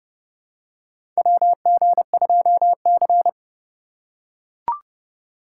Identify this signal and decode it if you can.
Morse code, heard as a keyed tone, about 30 wpm, 700 Hz. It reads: WG2C
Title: USA Call Signs - 3x Character Spacing - 30wpm